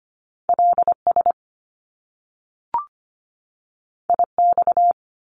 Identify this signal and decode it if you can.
Morse code, heard as a keyed tone, about 25 wpm, 700 Hz. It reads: LS IX
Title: Top Two Letter Combinations in Words 25wpm